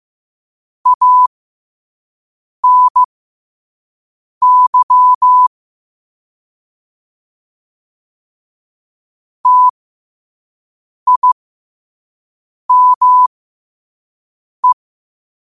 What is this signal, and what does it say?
Morse code, heard as a keyed tone, about 15 wpm, spaced as for 5 wpm, 1000 Hz. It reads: ANY TIME
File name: qotdfiles / 5wpm